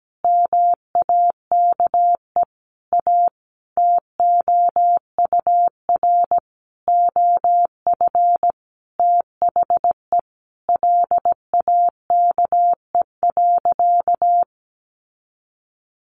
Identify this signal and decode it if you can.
Morse code, heard as a keyed tone, about 17 wpm, 700 Hz. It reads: MAKE A TOUR OF THE LAKE.
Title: Sentences from Top 1400 Words 17wpm